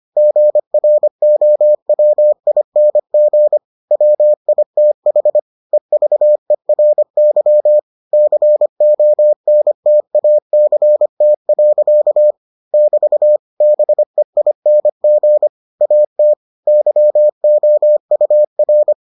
Morse code, heard as a keyed tone, about 25 wpm, 600 Hz. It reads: GROWING WITH EVERY CONTACT. <BT> BEING AT YOUR